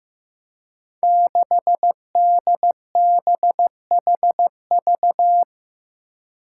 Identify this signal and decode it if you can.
Morse code, heard as a keyed tone, about 16 wpm, 700 Hz. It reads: 6DBHV
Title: Five Challenging Letter-Number Combinations 15wpm